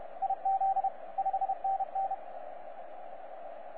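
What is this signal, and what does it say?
Morse code, heard as a keyed tone, about 31 wpm, 780 Hz, with radio noise. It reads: IZ 5NN
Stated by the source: 9U4U on 15 metre CW, DX station recording by DX World.net.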